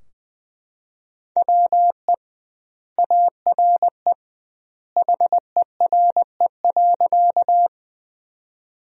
Morse code, heard as a keyed tone, about 21 wpm, 700 Hz. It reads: WE ARE HERE.
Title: 2x Word-Spacing - Sentences from Top 200 Words - 20wpm